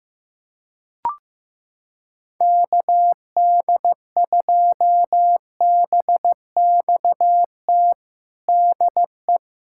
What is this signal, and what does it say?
Morse code, heard as a keyed tone, about 15 wpm, 700 Hz. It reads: KD2BXT DE
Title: USA Callsign DE Callsign 15wpm